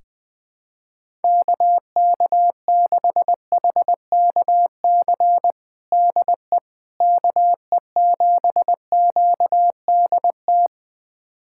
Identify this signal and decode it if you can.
Morse code, heard as a keyed tone, about 20 wpm, 700 Hz. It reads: KK6HKC DE KE7QDT
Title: USA Callsign DE Callsign - 6 Characters - 20wpm